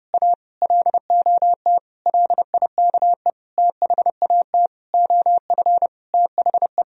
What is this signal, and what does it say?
Morse code, heard as a keyed tone, about 30 wpm, 700 Hz. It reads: A LOT LIKE THAT OF THE
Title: Sentences from Top 1000 Words - Review Part 2 - 30wpm